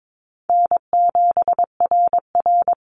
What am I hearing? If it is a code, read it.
Morse code, heard as a keyed tone, about 22 wpm, 700 Hz. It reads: N7RR
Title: NA SOTA Activator Callsigns 22wpm